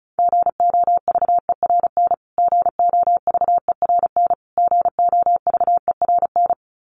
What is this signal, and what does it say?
Morse code, heard as a keyed tone, about 34 wpm, 700 Hz. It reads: GOVERN GOVERN GOVERN
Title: Top 500 Words Repeated 35wpm